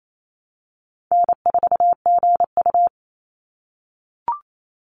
Morse code, heard as a keyed tone, about 28 wpm, 700 Hz. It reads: N4GU
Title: Call Signs - 28wpm